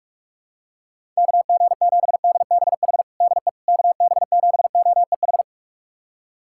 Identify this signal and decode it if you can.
Morse code, heard as a keyed tone, about 45 wpm, 700 Hz. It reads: KG7DBH DE KB7OEH
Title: USA Callsign DE Callsign - 6 Characters - 45wpm